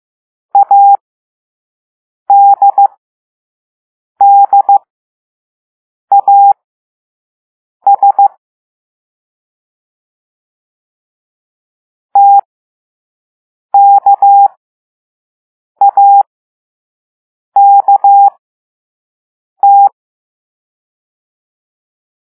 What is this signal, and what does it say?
Morse code, heard as a keyed tone, about 15 wpm, spaced as for 5 wpm, 800 Hz. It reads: ADDAS TKAKT